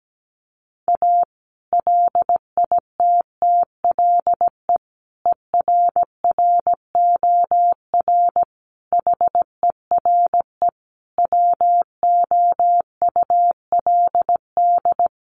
Morse code, heard as a keyed tone, about 17 wpm, 700 Hz. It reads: A LITTLE ERROR HERE WOULD
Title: A Sound Of Thunder 17wpm